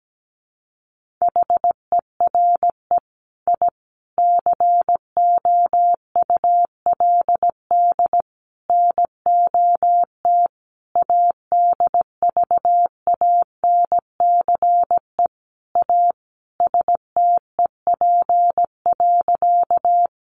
Morse code, heard as a keyed tone, about 17 wpm, 700 Hz. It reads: HERE I COULD NOT ADVANCE A STEP.
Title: Sentences from Top 1400 Words 17wpm